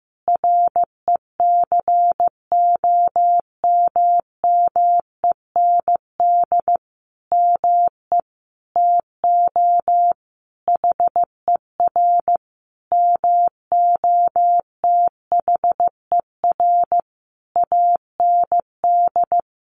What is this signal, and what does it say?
Morse code, heard as a keyed tone, about 15 wpm, 700 Hz. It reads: RECOMMEND ME TO HER MOTHER AND
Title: Sentences from Top 1200 Words 15wpm